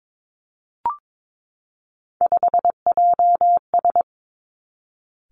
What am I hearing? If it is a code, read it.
Morse code, heard as a keyed tone, about 22 wpm, 700 Hz. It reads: 5JS
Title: Three Challenging Letter-Number Combinations 22wpm